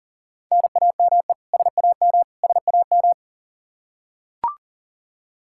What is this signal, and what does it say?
Morse code, heard as a keyed tone, about 40 wpm, 700 Hz. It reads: NAME SAM SAM
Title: Practice QSO - Exchange 1 - Beginning - 40wpm